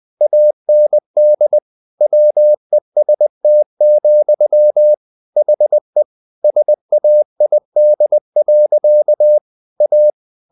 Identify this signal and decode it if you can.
Morse code, heard as a keyed tone, about 20 wpm, 600 Hz. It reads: AND WEST, HE SAID. A
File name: the old man and the sea part 2 20wpm